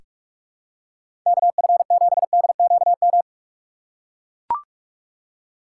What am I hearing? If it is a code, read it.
Morse code, heard as a keyed tone, about 42 wpm, 700 Hz. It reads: KF7DQM
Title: USA Call Signs - 6 Characters - 4x Character Spacing - 45wpm